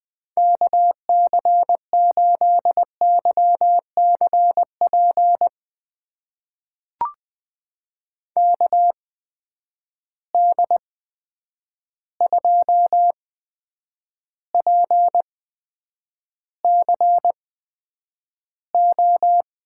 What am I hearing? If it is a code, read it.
Morse code, heard as a keyed tone, about 20 wpm, spaced as for 9 wpm, 700 Hz. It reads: KC8YCP KD2PCO
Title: USA Call Signs - 6 Characters - 8x Character Spacing - 20wpm